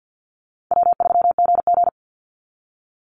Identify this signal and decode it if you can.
Morse code, heard as a keyed tone, about 50 wpm, 700 Hz. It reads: W3GG
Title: Easy Call Signs Repeated 2x - 50wpm